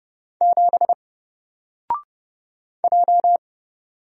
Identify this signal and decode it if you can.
Morse code, heard as a keyed tone, about 30 wpm, 700 Hz. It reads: 7 J
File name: Single-Letter-Number-30wpm